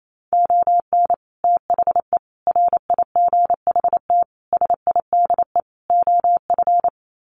Morse code, heard as a keyed tone, about 28 wpm, 690 Hz. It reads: ON THE RIGHT SIDE OF
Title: Sets of 5 Words Encore 28wpm